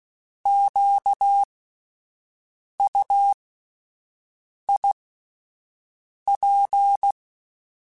Morse code, heard as a keyed tone, about 16 wpm, spaced as for 5 wpm, 800 Hz. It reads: QUIP
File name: Alphabetical / Alphabetical mp3s / Qs @5